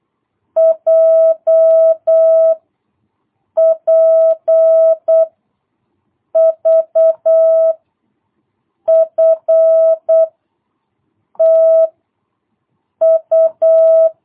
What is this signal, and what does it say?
Morse code, heard as a keyed tone, about 8 wpm, 650 Hz. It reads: J P V F T U
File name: jh8cbh / 2022cw